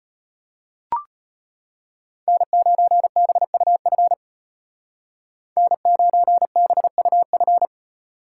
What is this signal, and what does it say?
Morse code, heard as a keyed tone, about 31 wpm, 700 Hz. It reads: N9BUF N9BUF
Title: Speed Racing - USA 1x3 Call Signs 25wpm